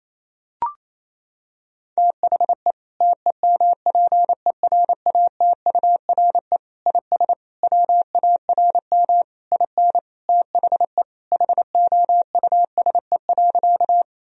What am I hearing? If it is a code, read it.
Morse code, heard as a keyed tone, about 28 wpm, 700 Hz. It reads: THE TEMPERATURE IS WARM IN THE HOUSE.
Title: Sentences from Top 1000 Words - Review Part 4 - 28wpm